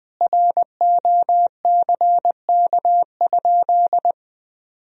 Morse code, heard as a keyed tone, about 20 wpm, 700 Hz. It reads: ROCK?
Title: Sentences from Top 1000 Words - Review Part 2 - 20wpm